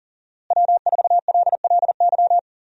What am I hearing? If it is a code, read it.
Morse code, heard as a keyed tone, about 40 wpm, 700 Hz. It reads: W4LLY
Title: USA Call Signs - 5 Characters - Repeated 2x - 40wpm